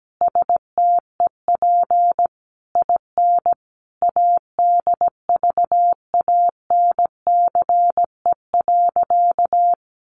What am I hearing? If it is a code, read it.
Morse code, heard as a keyed tone, about 17 wpm, 700 Hz. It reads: STEP IN ADVANCE.
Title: Sentences from Top 1400 Words 17wpm